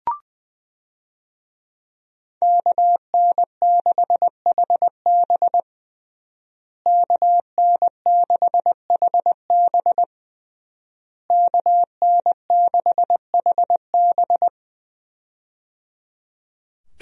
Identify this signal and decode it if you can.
Morse code, heard as a keyed tone, about 20 wpm, 700 Hz. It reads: KN6HB KN6HB KN6HB
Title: USA Call Signs - 5 Characters - Repeated 3x - 20wpm